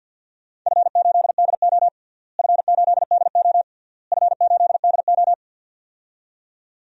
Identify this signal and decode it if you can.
Morse code, heard as a keyed tone, about 50 wpm, 700 Hz. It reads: F8DO F8DO F8DO
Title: DXCC-100 Call Signs - 4 Characters - Repeated 3x - 50wpm